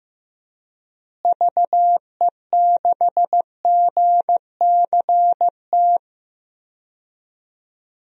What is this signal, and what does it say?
Morse code, heard as a keyed tone, about 15 wpm, 700 Hz. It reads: VE6GCT
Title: DXCC-100 Call Signs - 6 Characters - Repeated 2x - 15wpm